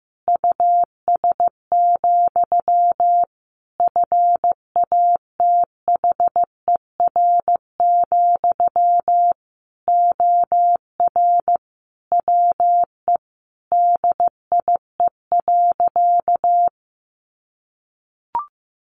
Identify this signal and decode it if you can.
Morse code, heard as a keyed tone, about 15 wpm, 700 Hz. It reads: US, FATHER, OR WE DIE.
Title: Sentences from Top 1300 Words 15wpm